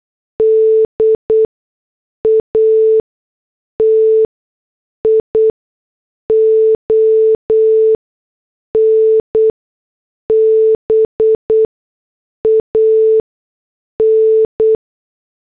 Morse code, heard as a keyed tone, about 8 wpm, spaced as for 6 wpm, 440 Hz. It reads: DATION BAN